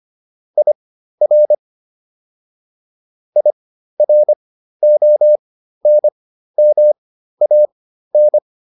Morse code, heard as a keyed tone, about 25 wpm, spaced as for 13 wpm, 600 Hz. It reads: IR IRONMAN